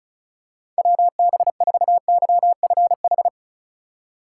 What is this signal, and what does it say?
Morse code, heard as a keyed tone, about 36 wpm, 700 Hz. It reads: WB4YFH
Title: USA Call Signs - 6 Characters - 3x Character Spacing - 35wpm